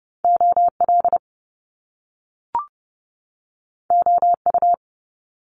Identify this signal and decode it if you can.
Morse code, heard as a keyed tone, about 30 wpm, 700 Hz. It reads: OL OU
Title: Top Two Letter Combinations in Words 30wpm